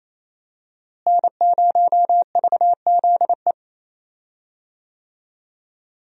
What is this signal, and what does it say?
Morse code, heard as a keyed tone, about 28 wpm, 700 Hz. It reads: N0VZE
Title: USA Call Signs - 5 Characters - Repeated 3x - 28wpm